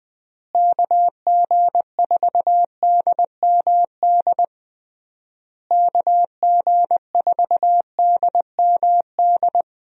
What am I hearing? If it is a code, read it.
Morse code, heard as a keyed tone, about 20 wpm, 700 Hz. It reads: KG4DMD KG4DMD
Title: USA Call Signs - 6 Characters - Repeated 3x - 20wpm